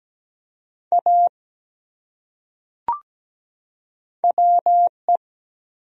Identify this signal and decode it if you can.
Morse code, heard as a keyed tone, about 17 wpm, 700 Hz. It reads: A WE
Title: Top 100 Words 17wpm